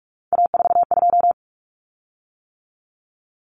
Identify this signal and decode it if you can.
Morse code, heard as a keyed tone, about 44 wpm, 700 Hz. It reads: A42
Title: Common Cut Numbers - 3-Digit 45wpm